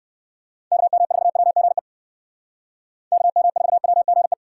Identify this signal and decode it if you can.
Morse code, heard as a keyed tone, about 58 wpm, 700 Hz. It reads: KM4WGE KM4WGE